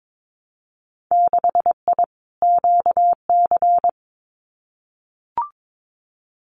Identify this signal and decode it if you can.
Morse code, heard as a keyed tone, about 22 wpm, 700 Hz. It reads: 6I QC
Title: Winter Field Day Exchanges Encore 22wpm